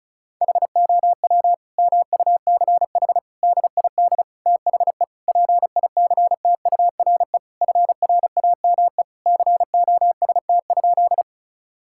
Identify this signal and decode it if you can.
Morse code, heard as a keyed tone, about 36 wpm, 700 Hz. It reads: HOW MUCH DID THE PICTURE FRAME COST?
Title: Sentences from Top 1000 Words - Review Part 6 - 35wpm